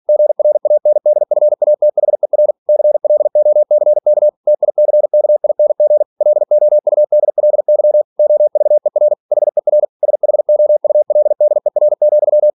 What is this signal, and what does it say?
Morse code, heard as a keyed tone, about 46 wpm, 600 Hz. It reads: GRANDFATHER CLOCK TICKING LOUDLY OVER HER SHOULDER,
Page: Never 47wpm